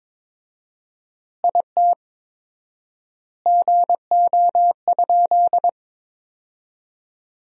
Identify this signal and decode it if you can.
Morse code, heard as a keyed tone, about 22 wpm, 700 Hz. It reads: IT GO?
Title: 4x Word-Spacing - Sentences from Top 200 Words - 22wpm